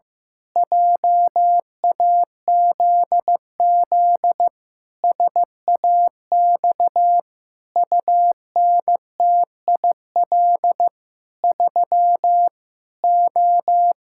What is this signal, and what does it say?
Morse code, heard as a keyed tone, about 15 wpm, 700 Hz. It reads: JAZZ SAX UNTIL 3 O
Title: Pangram Challenge v2 - 15wpm